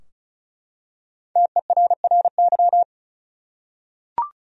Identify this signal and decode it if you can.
Morse code, heard as a keyed tone, about 36 wpm, 700 Hz. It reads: TERRY
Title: USA Common Names - Max 5 Letters - 35wpm